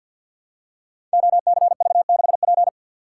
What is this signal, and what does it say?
Morse code, heard as a keyed tone, about 48 wpm, 700 Hz. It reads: OCV6P
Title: Code Groups - 5 Letter:Numbers - 50wpm